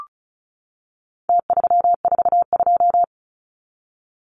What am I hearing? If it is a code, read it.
Morse code, heard as a keyed tone, about 34 wpm, 700 Hz. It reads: T342
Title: Numbers - Contest Style - 4-Digit 35wpm